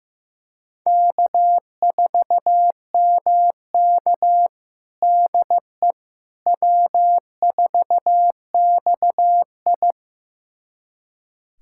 Morse code, heard as a keyed tone, about 16 wpm, 700 Hz. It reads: K4MK DE W4XI
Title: USA Callsign DE Callsign - 4 Characters - 15wpm